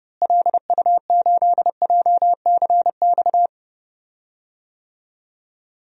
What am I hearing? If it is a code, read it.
Morse code, heard as a keyed tone, about 31 wpm, 700 Hz. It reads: LU8JCX
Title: DXCC-100 Call Signs - 6 Characters - Repeated 3x - 30wpm